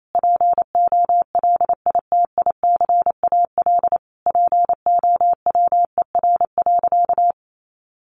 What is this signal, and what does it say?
Morse code, heard as a keyed tone, about 28 wpm, 700 Hz. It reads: POLITICAL POWER.
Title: Sentences from Top 900 Words 28wpm